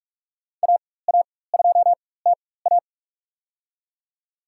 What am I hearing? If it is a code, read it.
Morse code, heard as a keyed tone, about 46 wpm, spaced as for 21 wpm, 690 Hz. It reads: AA2TA
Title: USA Call Signs - 5 Characters - 4x Character Spacing - 45wpm